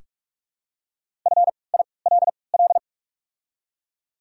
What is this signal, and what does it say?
Morse code, heard as a keyed tone, about 43 wpm, spaced as for 25 wpm, 700 Hz. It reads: FILL
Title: 3x Character Spacing - Top 500 Words - 45wpm